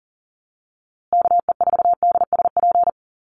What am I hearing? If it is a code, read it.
Morse code, heard as a keyed tone, about 40 wpm, 700 Hz. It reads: KE4DSP
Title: USA Call Signs Repeated 3x 40wpm